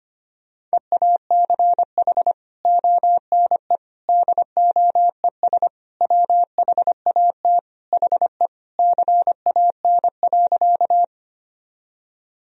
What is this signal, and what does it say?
Morse code, heard as a keyed tone, about 25 wpm, 700 Hz. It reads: EACH ONE DOES WHAT HE CAN.
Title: Sentences from Top 1000 Words - Review Part 1 - 25wpm